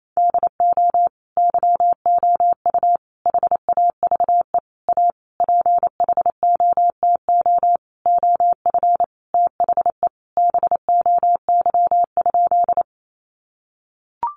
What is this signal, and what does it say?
Morse code, heard as a keyed tone, about 28 wpm, 700 Hz. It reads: DO YOU HAVE A PHOTO OF THE BOY?
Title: Sentences from Top 1200 Words 28wpm